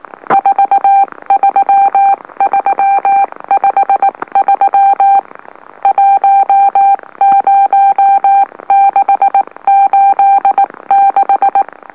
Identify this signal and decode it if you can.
Morse code, heard as a keyed tone, about 19 wpm, spaced as for 16 wpm, 790 Hz. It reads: 43353 10686
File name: M04- -2345utc-LOLO